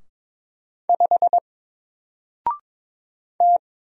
Morse code, heard as a keyed tone, about 22 wpm, 700 Hz. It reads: 5 T
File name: Single-Letter-Number-22wpm